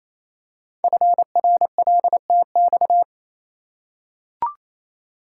Morse code, heard as a keyed tone, about 29 wpm, 700 Hz. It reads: FRLTX
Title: Code Groups - 5 Letters - 28wpm